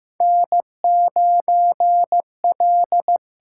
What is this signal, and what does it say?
Morse code, heard as a keyed tone, about 15 wpm, 700 Hz. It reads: N 9 L
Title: USA Call Signs - 8x Character Spacing - 15wpm